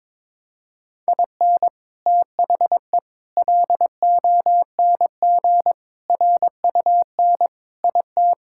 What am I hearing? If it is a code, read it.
Morse code, heard as a keyed tone, about 22 wpm, 700 Hz. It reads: IN THE LONG RUN IT
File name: practice / 22wpm / Sentences with 4-Word Phrases - 22wpm